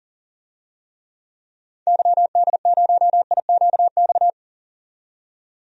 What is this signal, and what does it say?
Morse code, heard as a keyed tone, about 40 wpm, 700 Hz. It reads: YD0IQX
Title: DXCC-100 Call Signs - 6 Characters - 40wpm